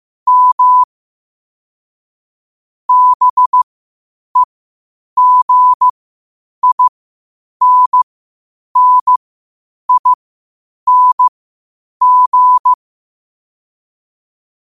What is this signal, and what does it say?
Morse code, heard as a keyed tone, about 15 wpm, spaced as for 8 wpm, 1000 Hz. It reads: M BEGINNING
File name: qotdfiles / 8wpm